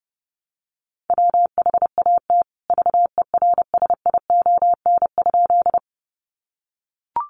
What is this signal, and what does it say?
Morse code, heard as a keyed tone, about 30 wpm, 700 Hz. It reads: WHAT VERSION?
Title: Sentences from Top 1400 Words 30wpm